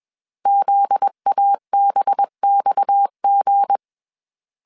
Morse code, heard as a keyed tone, about 21 wpm, 800 Hz. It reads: ZABXG
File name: CW / DET / 1 9 1500C S C 20wpm 2310